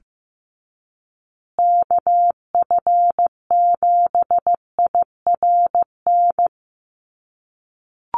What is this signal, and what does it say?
Morse code, heard as a keyed tone, about 15 wpm, 700 Hz. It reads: KF7IRN
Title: USA Call Signs Repeated 2x 15wpm